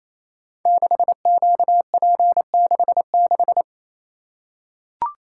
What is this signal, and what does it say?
Morse code, heard as a keyed tone, about 29 wpm, 700 Hz. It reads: 6QP66
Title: Code Groups - 5 Letter:Numbers - 28wpm